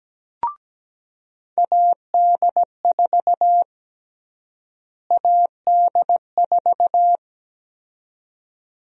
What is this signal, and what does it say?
Morse code, heard as a keyed tone, about 17 wpm, 700 Hz. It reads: AD4 AD4
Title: USA Call Sign Prefixes - Repeated 2x - 17wpm